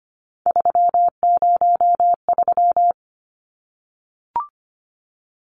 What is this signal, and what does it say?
Morse code, heard as a keyed tone, about 25 wpm, 700 Hz. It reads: 303
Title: Numbers - 3-Digit 25wpm